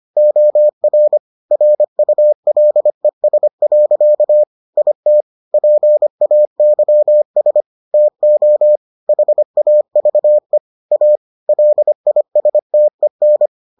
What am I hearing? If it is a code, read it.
Morse code, heard as a keyed tone, about 25 wpm, 600 Hz. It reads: OR RULES. IT PAYS TO HAVE A LISTEN